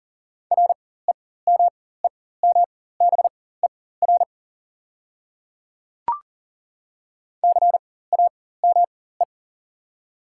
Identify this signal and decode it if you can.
Morse code, heard as a keyed tone, about 40 wpm, spaced as for 19 wpm, 700 Hz. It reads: REMEMBER CAME
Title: 4x Character Spacing - Top 500 Words - 40wpm